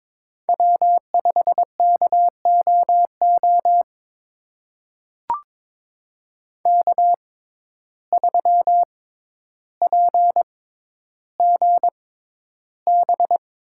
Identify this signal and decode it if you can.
Morse code, heard as a keyed tone, about 22 wpm, spaced as for 11 wpm, 700 Hz. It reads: W5KOO K3PGB
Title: USA Call Signs - 5 Characters - 6x Character Spacing - 22wpm